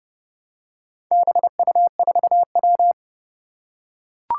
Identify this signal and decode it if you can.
Morse code, heard as a keyed tone, about 30 wpm, 700 Hz. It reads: BU4W
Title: Four Challenging Letter-Number Combinations 30wpm